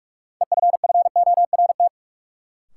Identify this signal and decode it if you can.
Morse code, heard as a keyed tone, about 46 wpm, 700 Hz. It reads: EFFORT
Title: Sets of 4 Words - New Word Drill - 45wpm